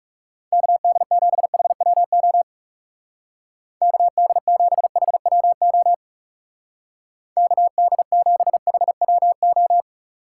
Morse code, heard as a keyed tone, about 36 wpm, 700 Hz. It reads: KD7HWO KD7HWO KD7HWO
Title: Speed Racing - USA 2x3 Call Signs 30wpm